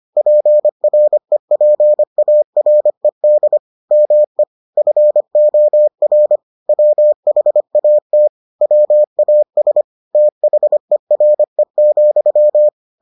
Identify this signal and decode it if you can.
Morse code, heard as a keyed tone, about 25 wpm, 600 Hz. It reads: PREPARED ME FOR WHAT WAS THERE,